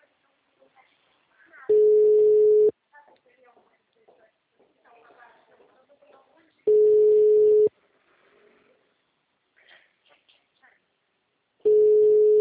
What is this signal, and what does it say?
Ringback tone (Europe, 425 Hz), with speech.